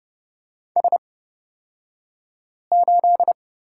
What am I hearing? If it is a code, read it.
Morse code, heard as a keyed tone, about 30 wpm, spaced as for 5 wpm, 700 Hz. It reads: S8
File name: Lesson 36d - BT - Two Characters 30wpm